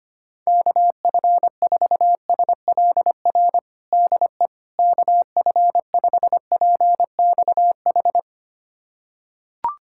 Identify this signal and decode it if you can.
Morse code, heard as a keyed tone, about 25 wpm, 700 Hz. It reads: KF4SLR DE KF5PXH
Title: USA Callsign DE Callsign - 6 Characters - 25wpm